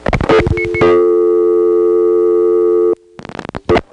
Dial tone (North America), 350 + 440 Hz.
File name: phonetrips / SmyrnaGA3HQ